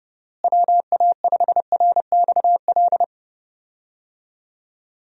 Morse code, heard as a keyed tone, about 32 wpm, 700 Hz. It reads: WA5RXL
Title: USA Call Signs - 6 Characters - Repeated 3x - 30wpm